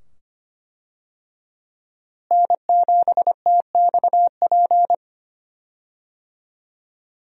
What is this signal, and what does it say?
Morse code, heard as a keyed tone, about 25 wpm, 700 Hz. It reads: N7TXP
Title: USA Call Signs - 5 Characters - Repeated 2x - 25wpm